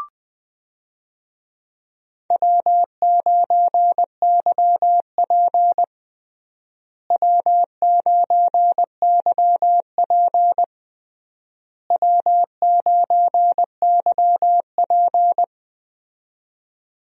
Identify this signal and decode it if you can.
Morse code, heard as a keyed tone, about 20 wpm, 700 Hz. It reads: W9YP W9YP W9YP
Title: USA Call Signs - 4 Characters - Repeated 3x - 20wpm